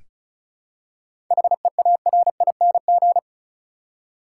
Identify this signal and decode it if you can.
Morse code, heard as a keyed tone, about 36 wpm, 700 Hz. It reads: HEARING
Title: Seven Letter Words - 35wpm